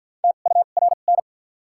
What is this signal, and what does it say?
Morse code, heard as a keyed tone, about 53 wpm, spaced as for 37 wpm, 700 Hz. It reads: TURN
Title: 2x Character Spacing - Top 200 Words - 50wpm